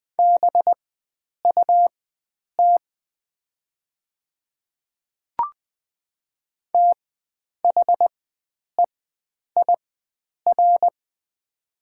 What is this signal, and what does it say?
Morse code, heard as a keyed tone, about 20 wpm, spaced as for 9 wpm, 700 Hz. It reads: BUT THEIR
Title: 4x Character Spacing - Top 100 Words - 20wpm